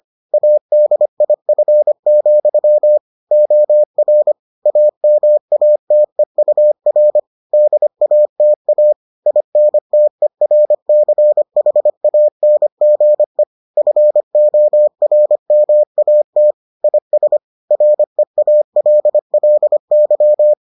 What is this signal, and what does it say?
Morse code, heard as a keyed tone, about 25 wpm, 600 Hz. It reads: ADIF, OR AMATEUR DATA INTERCHANGE FORMAT IS REALLY